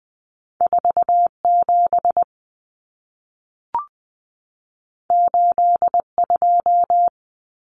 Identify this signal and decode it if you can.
Morse code, heard as a keyed tone, about 20 wpm, 700 Hz. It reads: 47 82
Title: Common Cut Numbers - 2-Digit 20wpm